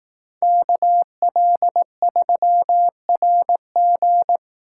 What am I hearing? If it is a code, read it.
Morse code, heard as a keyed tone, about 18 wpm, 700 Hz. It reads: KL3RG
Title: Speed Racing - USA 2x2 Call Signs 15wpm